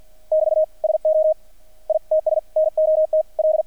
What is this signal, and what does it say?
Morse code, heard as a keyed tone, about 48 wpm, 630 Hz. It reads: <BT> SO ITS NOT L
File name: archived / CRSnet